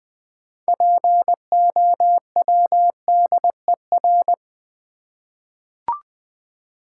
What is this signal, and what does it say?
Morse code, heard as a keyed tone, about 20 wpm, 700 Hz. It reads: POWDER
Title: 6 Letter Words - First 3 Letters Given - 20wpm